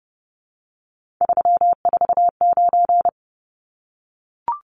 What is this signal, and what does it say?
Morse code, heard as a keyed tone, about 29 wpm, 700 Hz. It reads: 349
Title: Numbers - 3-Digit 30wpm